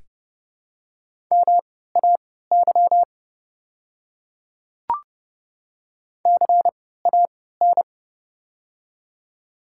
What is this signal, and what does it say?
Morse code, heard as a keyed tone, about 30 wpm, spaced as for 17 wpm, 700 Hz. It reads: MAY CAN
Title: 3x Character Spacing - Top 100 Words - 30wpm